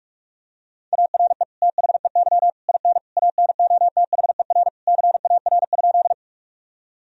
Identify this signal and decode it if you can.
Morse code, heard as a keyed tone, about 46 wpm, 700 Hz. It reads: ARE THEY IN ANOTHER CAR?
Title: Sentences from Top 1000 Words - Review Part 1 - 45wpm